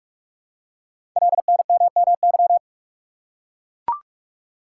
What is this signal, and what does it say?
Morse code, heard as a keyed tone, about 44 wpm, 700 Hz. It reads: LNMMY
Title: Code Groups - 5 Letters - 45wpm